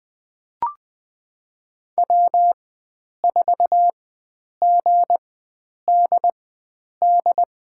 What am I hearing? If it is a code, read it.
Morse code, heard as a keyed tone, about 20 wpm, spaced as for 9 wpm, 700 Hz. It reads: W4GDD
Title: USA Call Signs - 4x Character Spacing - 20wpm